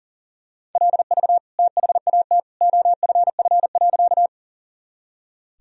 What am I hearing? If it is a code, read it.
Morse code, heard as a keyed tone, about 40 wpm, 700 Hz. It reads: LV THAT OFF.
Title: Sentences from Top 400 Words and Abbreviations 40wpm